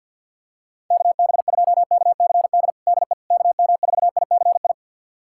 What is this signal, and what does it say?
Morse code, heard as a keyed tone, about 50 wpm, 700 Hz. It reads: KB2KCD DE KM4ICI
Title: USA Callsign DE Callsign 50wpm